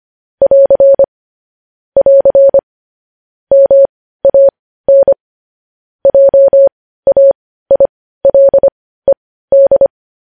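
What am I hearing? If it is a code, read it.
Morse code, heard as a keyed tone, about 25 wpm, spaced as for 15 wpm, 550 Hz. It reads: <AR> <AR> MAN JAILED
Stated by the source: Morse code practice of news headlines at 15 wpm, opening with vvv.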